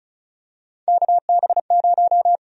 Morse code, heard as a keyed tone, about 34 wpm, 700 Hz. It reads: KB0
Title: USA Call Sign Prefixes - 35wpm